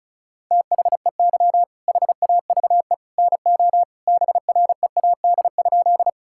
Morse code, heard as a keyed tone, about 36 wpm, 700 Hz. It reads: THEY HAVE NO BREAD?
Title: Sentences from Top 1600 Words 35wpm